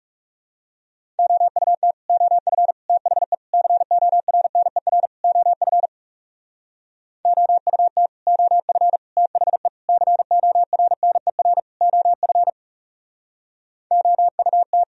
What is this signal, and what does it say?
Morse code, heard as a keyed tone, about 38 wpm, 700 Hz. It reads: OUT OF THE CORNER OF OUT OF THE CORNER OF OUT